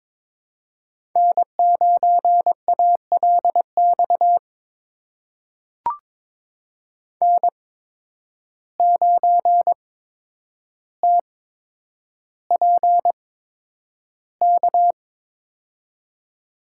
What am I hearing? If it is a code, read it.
Morse code, heard as a keyed tone, about 22 wpm, spaced as for 9 wpm, 700 Hz. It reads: N9ALX N9TPK
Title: USA Call Signs - 5 Characters - 8x Character Spacing - 22wpm